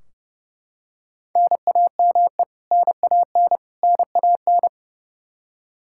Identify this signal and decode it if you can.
Morse code, heard as a keyed tone, about 30 wpm, 700 Hz. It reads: NAME NAN NAN
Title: Practice QSO - Exchange 1 - Beginning - 30wpm